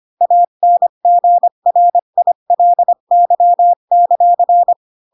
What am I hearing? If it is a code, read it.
Morse code, heard as a keyed tone, about 25 wpm, 700 Hz. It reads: ANGRILY;
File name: Pinocchio / Pinocchio ch3 20 25 ik1hge